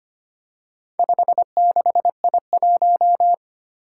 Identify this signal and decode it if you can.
Morse code, heard as a keyed tone, about 25 wpm, 700 Hz. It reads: 56I1
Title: Four Challenging Letter-Number Combinations 25wpm